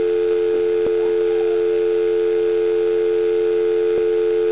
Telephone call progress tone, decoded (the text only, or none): dial tone